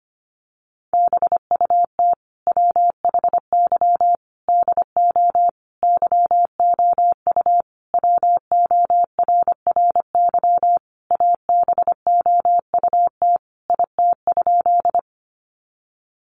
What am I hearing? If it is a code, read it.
Morse code, heard as a keyed tone, about 25 wpm, 700 Hz. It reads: BUT WHY DO YOU WORRY ABOUT IT?
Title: Sentences from Top 1100 Words 25wpm